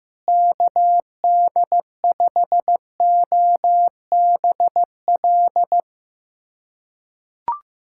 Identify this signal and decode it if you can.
Morse code, heard as a keyed tone, about 15 wpm, 700 Hz. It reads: KD5OBL
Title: USA Callsign DE Callsign 15wpm